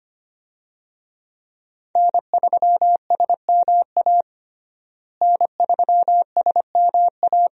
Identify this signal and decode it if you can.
Morse code, heard as a keyed tone, about 25 wpm, 700 Hz. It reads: N3SMA N3SMA
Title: USA Call Signs - 5 Characters - Repeated 2x - 25wpm